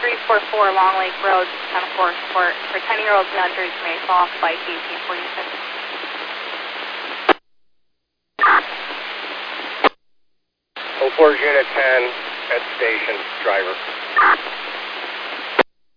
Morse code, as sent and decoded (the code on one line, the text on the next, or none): none